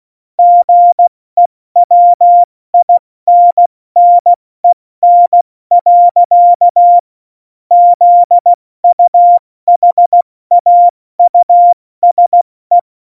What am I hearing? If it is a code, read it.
Morse code, heard as a keyed tone, about 16 wpm, 700 Hz. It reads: GEWINNEN. ZUHAUSE
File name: Morsekurs / Witze / Witz 18 August 2020, 70 F80